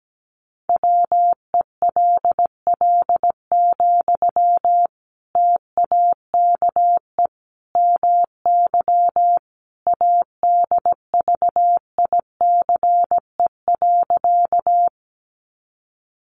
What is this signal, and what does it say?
Morse code, heard as a keyed tone, about 17 wpm, 700 Hz. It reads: WELL, TAKE MY ADVICE.
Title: Sentences from Top 1100 Words 17wpm